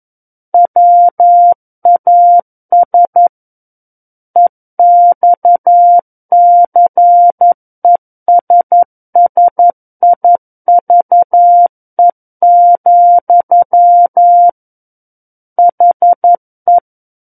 Morse code, heard as a keyed tone, about 11 wpm, 700 Hz. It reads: WAS EXCESSIVE, HE
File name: 11 WPM